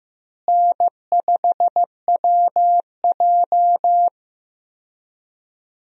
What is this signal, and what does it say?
Morse code, heard as a keyed tone, about 15 wpm, 700 Hz. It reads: N5WJ
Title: USA Call Signs - 4 Characters - Repeated 2x - 15wpm